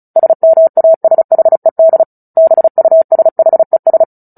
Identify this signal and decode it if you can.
Morse code, heard as a keyed tone, about 35 wpm, 650 Hz. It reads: SMASHED BUSHES
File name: Book Two / 09-Chapter 9 - 35 WPM